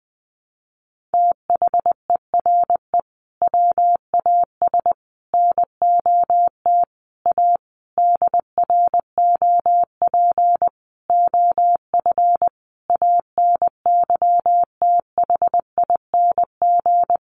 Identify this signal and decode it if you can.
Morse code, heard as a keyed tone, about 19 wpm, 700 Hz. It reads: THERE WAS NOT A DROP OF ANYTHING
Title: Sentences from Top 1500 Words 20wpm